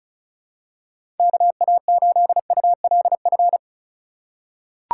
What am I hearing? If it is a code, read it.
Morse code, heard as a keyed tone, about 36 wpm, 700 Hz. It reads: KA8ULF
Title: USA Call Signs - 8x Character Spacing - 35wpm